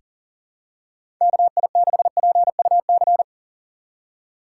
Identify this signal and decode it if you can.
Morse code, heard as a keyed tone, about 39 wpm, 700 Hz. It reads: KI6PUC
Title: USA Call Signs - 3x Character Spacing - 40wpm